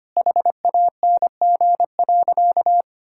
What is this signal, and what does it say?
Morse code, heard as a keyed tone, about 25 wpm, 700 Hz. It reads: HANG.
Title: Sentences from Top 1600 Words 25wpm